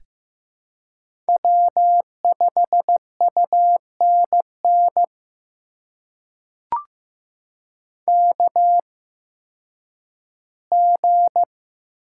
Morse code, heard as a keyed tone, about 15 wpm, spaced as for 10 wpm, 700 Hz. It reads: W5UNN KG
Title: USA Call Signs - 8x Character Spacing - 15wpm